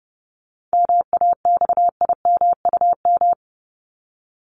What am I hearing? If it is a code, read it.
Morse code, heard as a keyed tone, about 30 wpm, 700 Hz. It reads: MAXIMUM
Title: Top 1001-1250 Words 30wpm